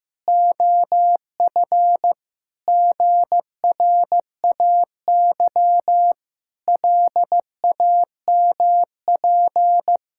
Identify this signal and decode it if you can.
Morse code, heard as a keyed tone, about 15 wpm, 700 Hz. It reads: OF GRAY LAMP
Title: Pangram Challenge v2 - 15wpm